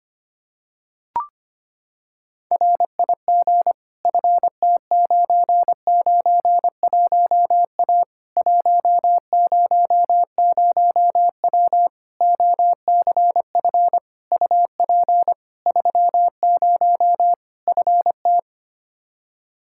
Morse code, heard as a keyed tone, about 25 wpm, 700 Hz. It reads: RIG FT991A 100W OCF UP 30 FT
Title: Practice QSO - Exchange 2 - Info - 25wpm